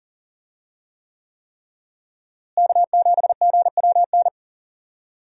Morse code, heard as a keyed tone, about 40 wpm, 700 Hz. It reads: K7GWN
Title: USA Call Signs - 5 Characters - Repeated 3x - 40wpm